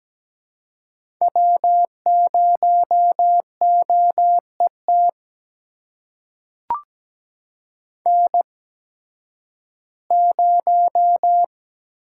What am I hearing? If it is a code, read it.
Morse code, heard as a keyed tone, about 17 wpm, spaced as for 11 wpm, 700 Hz. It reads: W0OET N0
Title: USA Call Signs - 5 Characters - 8x Character Spacing - 17wpm